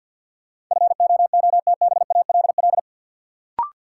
Morse code, heard as a keyed tone, about 47 wpm, 700 Hz. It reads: FOOTBALL
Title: Sentences from Top 1600 Words - New Word Drill - 50wpm